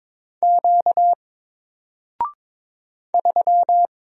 Morse code, heard as a keyed tone, about 22 wpm, 700 Hz. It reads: Q 3
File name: Single-Letter-Number-22wpm